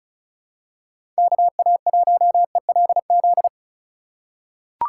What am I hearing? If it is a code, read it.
Morse code, heard as a keyed tone, about 36 wpm, 700 Hz. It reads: KA1ELZ
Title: USA Call Signs - 6 Characters - 2x Character Spacing - 35wpm